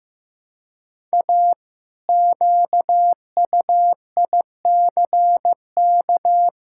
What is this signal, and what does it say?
Morse code, heard as a keyed tone, about 16 wpm, 700 Hz. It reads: A QUICK
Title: Pangram Challenge v2 - 15wpm